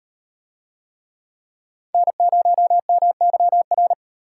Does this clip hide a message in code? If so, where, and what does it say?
Yes: Morse code from 1.9 to 3.9 s, heard as a keyed tone, about 37 wpm, 700 Hz. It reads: N0MYR